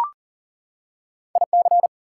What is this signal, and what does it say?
Morse code, heard as a keyed tone, about 40 wpm, 700 Hz. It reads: IC
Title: Prefixes and Suffixes 40wpm